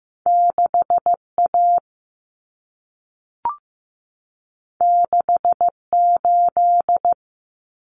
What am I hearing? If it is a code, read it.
Morse code, heard as a keyed tone, about 15 wpm, 700 Hz. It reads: 6A 68
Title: Common Cut Numbers - 2-Digit 15wpm